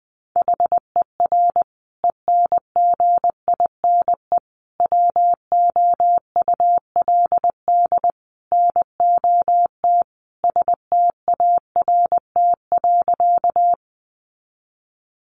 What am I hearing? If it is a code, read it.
Morse code, heard as a keyed tone, about 20 wpm, 700 Hz. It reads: HER ENGINE WOULD NOT START.
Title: Sentences from Top 1500 Words 20wpm